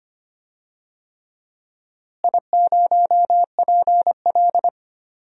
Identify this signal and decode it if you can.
Morse code, heard as a keyed tone, about 25 wpm, 700 Hz. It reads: I0PL
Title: DXCC-100 Call Signs - 4 Characters - Repeated 3x - 25wpm